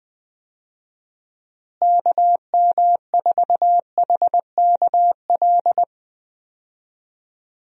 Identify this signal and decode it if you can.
Morse code, heard as a keyed tone, about 20 wpm, 700 Hz. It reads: KM4HKL
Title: USA Call Signs - 6 Characters - Repeated 2x - 20wpm